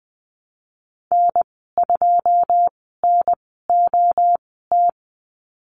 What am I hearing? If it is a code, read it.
Morse code, heard as a keyed tone, about 19 wpm, spaced as for 14 wpm, 700 Hz. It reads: N2NOT
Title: 128 Lesson 21g - Number 6 - Call Signs 20wpm